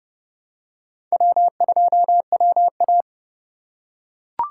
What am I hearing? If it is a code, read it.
Morse code, heard as a keyed tone, about 30 wpm, 700 Hz. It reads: W2WA
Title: USA Call Signs - 4 Characters - 3x Character Spacing - 30wpm